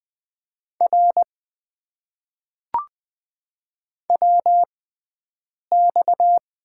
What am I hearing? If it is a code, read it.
Morse code, heard as a keyed tone, about 20 wpm, spaced as for 7 wpm, 700 Hz. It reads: R WX
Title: Top 90 QSO Elements - 6x Character Spacing - 20wpm